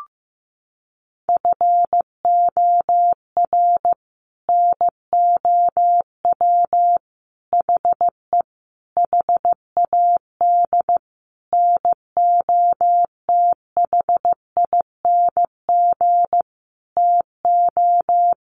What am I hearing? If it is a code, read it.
Morse code, heard as a keyed tone, about 14 wpm, 700 Hz. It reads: FOR NOW HE HAD NOTHING TO
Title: Sentences from Top 1100 Words 15wpm